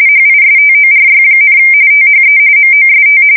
RTTY, about 45.45 baud. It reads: HISTORY, THE MOST
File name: fsk170 / USS-New-Jersey